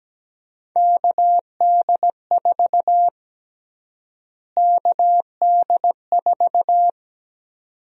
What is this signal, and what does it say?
Morse code, heard as a keyed tone, about 17 wpm, 700 Hz. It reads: KD4 KD4
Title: USA Call Sign Prefixes - Repeated 2x - 17wpm